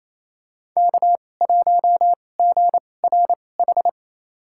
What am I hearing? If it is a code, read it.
Morse code, heard as a keyed tone, about 28 wpm, spaced as for 20 wpm, 700 Hz. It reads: K1GRH
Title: USA Call Signs - 5 Characters - 2x Character Spacing - 28wpm